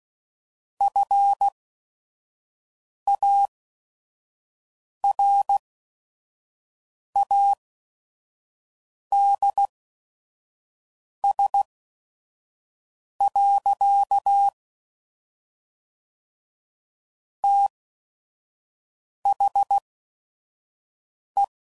Morse code, heard as a keyed tone, about 16 wpm, spaced as for 5 wpm, 800 Hz. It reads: FARADS. THE
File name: CW Code Groups @5WPM / Technical Subjects / XC